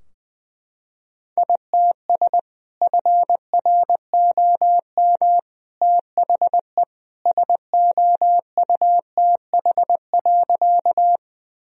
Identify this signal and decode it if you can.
Morse code, heard as a keyed tone, about 21 wpm, 700 Hz. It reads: ITS FROM THE SOUTH.
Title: Sentences from Top 1000 Words - Review Part 2 - 20wpm